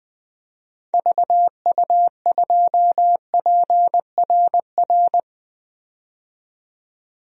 Morse code, heard as a keyed tone, about 20 wpm, 700 Hz. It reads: VU2PRR
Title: DXCC-100 Call Signs - 6 Characters - 20wpm